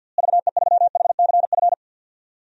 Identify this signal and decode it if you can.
Morse code, heard as a keyed tone, about 53 wpm, 700 Hz. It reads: VE3HCF
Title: DXCC-100 Call Signs - 6 Characters - Repeated 3x - 50wpm